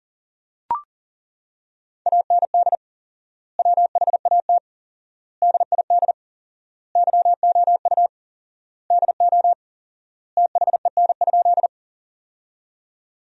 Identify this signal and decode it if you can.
Morse code, heard as a keyed tone, about 37 wpm, 700 Hz. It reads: AND WHAT DID YOU DO THEN?
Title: 4x Word-Spacing - Sentences from Top 100 Words - 40wpm